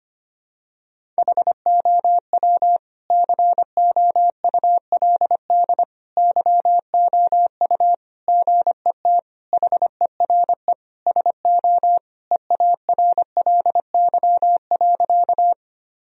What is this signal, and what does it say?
Morse code, heard as a keyed tone, about 25 wpm, 700 Hz. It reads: HOW COULD YOU GET HERE SO EARLY.
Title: Sentences from Top 1000 Words - Review Part 2 - 25wpm